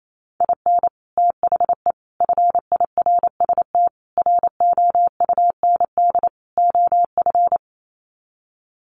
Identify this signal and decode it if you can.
Morse code, heard as a keyed tone, about 28 wpm, 700 Hz. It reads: IN THE FIRST ROUND OF
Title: Sets of 5 Words Encore 28wpm